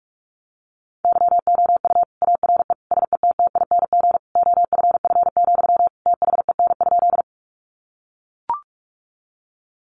Morse code, heard as a keyed tone, about 44 wpm, 700 Hz. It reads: YOU ARE SETTING OFF, THEN?
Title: Sentences from Top 1300 Words 45wpm